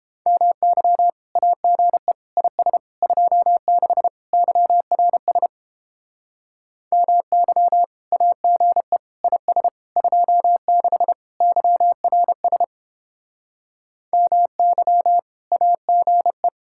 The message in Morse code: MY AGE IS 26 YRS MY AGE IS 26 YRS MY AGE

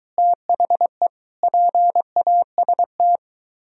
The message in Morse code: THE PAST

